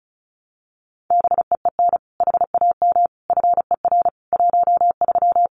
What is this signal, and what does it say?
Morse code, heard as a keyed tone, about 34 wpm, 700 Hz. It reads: BEEN HAM FER 13
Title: Common QSO Phrases 35wpm